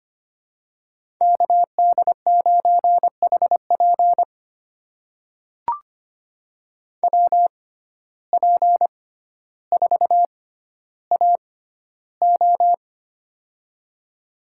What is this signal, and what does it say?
Morse code, heard as a keyed tone, about 25 wpm, spaced as for 13 wpm, 690 Hz. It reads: KD9HP WP4AO
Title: USA Call Signs - 5 Characters - 6x Character Spacing - 25wpm